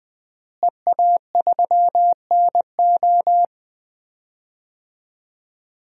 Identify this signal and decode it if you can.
Morse code, heard as a keyed tone, about 21 wpm, 700 Hz. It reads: EA3NO
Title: DXCC-100 Call Signs - 5 Characters - Repeated 2x - 20wpm